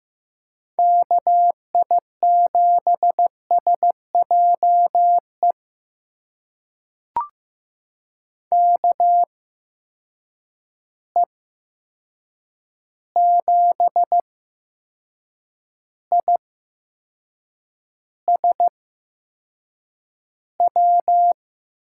Morse code, heard as a keyed tone, about 15 wpm, spaced as for 6 wpm, 700 Hz. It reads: KI7SJE KE7ISW
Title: USA Call Signs - 6 Characters - 8x Character Spacing - 15wpm